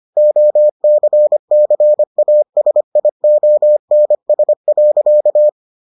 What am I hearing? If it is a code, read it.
Morse code, heard as a keyed tone, about 25 wpm, 600 Hz. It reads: OCCASIONS.